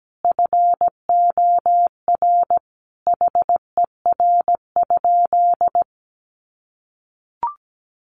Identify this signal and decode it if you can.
Morse code, heard as a keyed tone, about 17 wpm, 700 Hz. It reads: FOR HER?
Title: Sentences from Top 1500 Words 17wpm